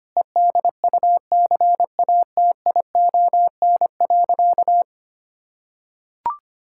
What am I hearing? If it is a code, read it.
Morse code, heard as a keyed tone, about 25 wpm, 700 Hz. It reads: EDUCATION.
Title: Sentences from Top 1000 Words - Review Part 6 - 25wpm